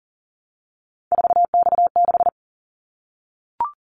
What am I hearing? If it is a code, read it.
Morse code, heard as a keyed tone, about 40 wpm, 700 Hz. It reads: 4X6
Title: Callsign Prefixes for Top 50 DXCC - 40wpm